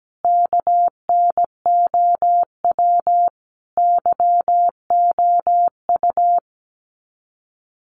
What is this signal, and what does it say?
Morse code, heard as a keyed tone, about 17 wpm, 700 Hz. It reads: KNOW YOU
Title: Sets of 5 Words Encore2 17wpm